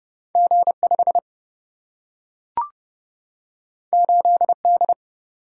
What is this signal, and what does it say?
Morse code, heard as a keyed tone, about 30 wpm, 700 Hz. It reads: G 5 8 D
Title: Single-Letter Single-Number 30wpm